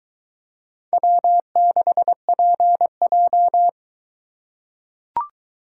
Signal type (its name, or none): Morse code